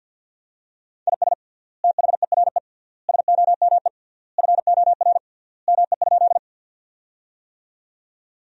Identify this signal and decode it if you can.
Morse code, heard as a keyed tone, about 50 wpm, 700 Hz. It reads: IS THERE SOME FOR ME?
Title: 3x Word-Spacing - Sentences from Top 200 Words - 50wpm